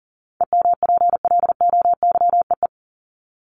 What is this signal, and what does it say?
Morse code, heard as a keyed tone, about 40 wpm, 700 Hz. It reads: EMPLOYEE
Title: Top 1501-1750 Words 40wpm